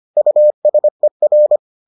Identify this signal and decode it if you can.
Morse code, heard as a keyed tone, about 25 wpm, 600 Hz. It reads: USER